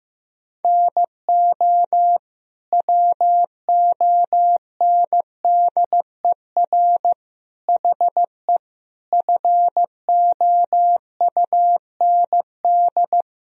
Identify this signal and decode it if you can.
Morse code, heard as a keyed tone, about 15 wpm, 700 Hz. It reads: NO WONDER HE FOUND
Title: Sentences from Top 1000 Words - Review Part 3 - 15wpm